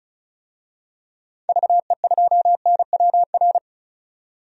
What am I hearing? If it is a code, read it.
Morse code, heard as a keyed tone, about 34 wpm, 700 Hz. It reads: VE2NWR
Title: DXCC-100 Call Signs - 6 Characters - 35wpm